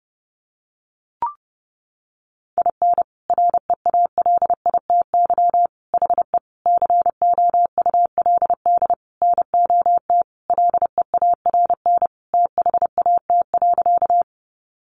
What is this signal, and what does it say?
Morse code, heard as a keyed tone, about 30 wpm, 700 Hz. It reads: IN REALITY HE COULD NOT LEARN THAT.
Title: Sentences from Top 1200 Words 30wpm